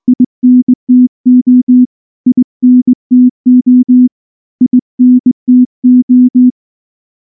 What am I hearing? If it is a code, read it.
Morse code, heard as a keyed tone, about 20 wpm, 250 Hz. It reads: INTO INTO INTO
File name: morse-training-20WPM